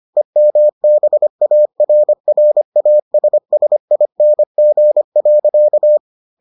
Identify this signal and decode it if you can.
Morse code, heard as a keyed tone, about 25 wpm, 600 Hz. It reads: EMBARRASSING.